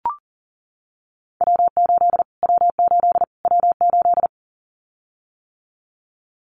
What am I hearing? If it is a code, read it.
Morse code, heard as a keyed tone, about 40 wpm, 700 Hz. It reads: W8 W8 W8
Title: Callsign Prefixes for Top 25 DXCC - Repeated 3x - 40wpm